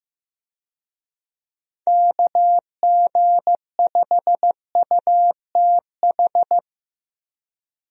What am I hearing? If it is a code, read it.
Morse code, heard as a keyed tone, about 15 wpm, 700 Hz. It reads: KG5UTH
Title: USA Call Signs - 6 Characters - Repeated 3x - 15wpm